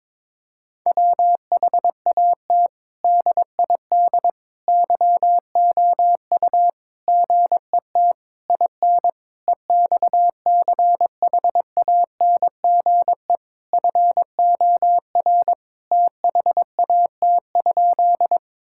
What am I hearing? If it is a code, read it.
Morse code, heard as a keyed tone, about 22 wpm, 700 Hz. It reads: WHAT DID YOU GET IN EXCHANGE FOR THAT?
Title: Sentences from Top 1000 Words - Review Part 6 - 22wpm